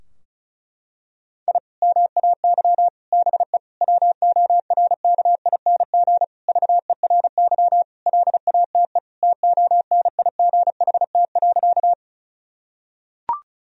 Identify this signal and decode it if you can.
Morse code, heard as a keyed tone, about 36 wpm, 700 Hz. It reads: I MAY BE WORKING VERY LATE TONIGHT.
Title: Sentences from Top 1000 Words - Review Part 4 - 35wpm